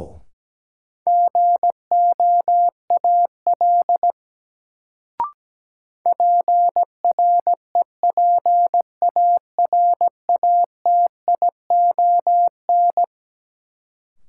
Morse code, heard as a keyed tone, about 17 wpm, 700 Hz. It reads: GOAL PREPARATION
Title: Sentences from Top 1400 Words - New Word Drill Alt - 17wpm